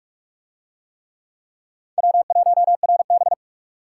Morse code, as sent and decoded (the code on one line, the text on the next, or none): .-- .---- .-. -...
W1RB